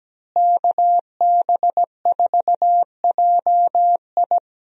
Morse code, heard as a keyed tone, about 17 wpm, 700 Hz. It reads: KB4JI
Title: USA Call Signs - 5 Characters - Repeated 2x - 17wpm